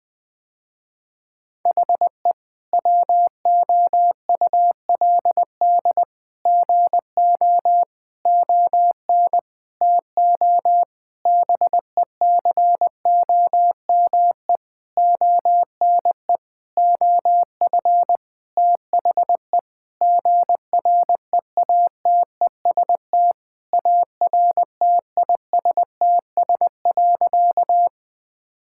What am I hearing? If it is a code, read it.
Morse code, heard as a keyed tone, about 21 wpm, 700 Hz. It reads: HE WOULD GO ON TO BECOME ONE OF THE GREATEST ARTISTS.
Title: Sentences with 8-Word Phrases - Spoken Beginning - 20wpm